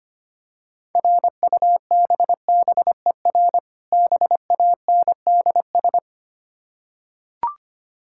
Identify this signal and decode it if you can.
Morse code, heard as a keyed tone, about 25 wpm, 700 Hz. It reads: RUBBER BANDS
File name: practice / 25wpm / Difficult Words with Hints - 25wpm